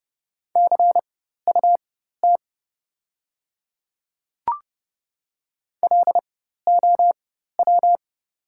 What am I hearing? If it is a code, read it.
Morse code, heard as a keyed tone, about 30 wpm, spaced as for 14 wpm, 700 Hz. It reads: CUT LOW
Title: 4x Character Spacing - Top 300 Words - 30wpm